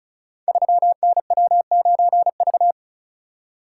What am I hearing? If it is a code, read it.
Morse code, heard as a keyed tone, about 36 wpm, 700 Hz. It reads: 3NW9V
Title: Code Groups - 5 Letter:Numbers - 35wpm